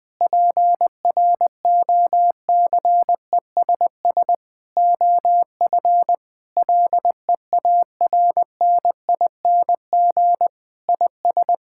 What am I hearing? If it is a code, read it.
Morse code, heard as a keyed tone, about 20 wpm, 700 Hz. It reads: PROCESS OF LEARNING IS
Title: Sentences with 3-Word Phrases - 20wpm